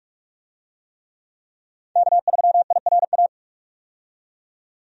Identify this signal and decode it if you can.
Morse code, heard as a keyed tone, about 46 wpm, 700 Hz. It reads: K3IRA
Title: USA Call Signs - 5 Characters - 45wpm